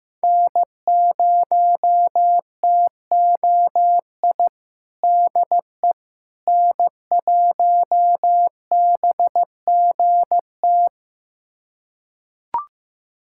Morse code, heard as a keyed tone, about 14 wpm, 700 Hz. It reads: N0TOI DE N1BGT
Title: USA Callsign DE Callsign - 5 Characters - 15wpm